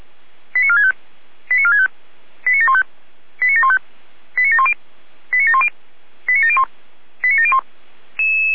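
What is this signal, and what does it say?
Five-tone selective call (ZVEI), heard as quick runs of five tones, tones of about 70 ms. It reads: 89465 89465 89715 89715 89710 89710 89701 89701
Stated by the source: ZVEI five-tone selective call received by SDR.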